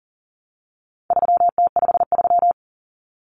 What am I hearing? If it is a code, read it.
Morse code, heard as a keyed tone, about 40 wpm, 700 Hz. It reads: 3T53
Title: Common Cut Numbers - 4-Digit 40wpm